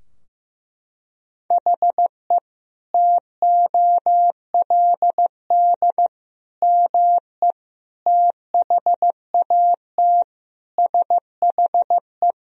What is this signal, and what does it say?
Morse code, heard as a keyed tone, about 15 wpm, 700 Hz. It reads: HE TOLD ME THAT SHE
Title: Sentences from Top 1000 Words - Review Part 2 - 15wpm